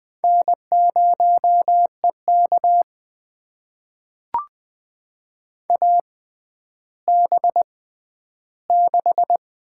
Morse code, heard as a keyed tone, about 20 wpm, spaced as for 11 wpm, 700 Hz. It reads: N0EK AB6
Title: USA Call Signs - 6x Character Spacing - 20wpm